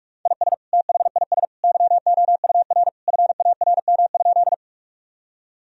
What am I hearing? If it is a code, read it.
Morse code, heard as a keyed tone, about 46 wpm, 700 Hz. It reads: IS THIS YOUR FARM?
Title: Sentences from Top 1000 Words - Review Part 2 - 45wpm